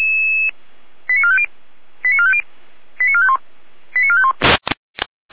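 Five-tone selective call (ZVEI), heard as quick runs of five tones, tones of about 70 ms. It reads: 89450 89450 89451 89451
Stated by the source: ZVEI five-tone call received by SDR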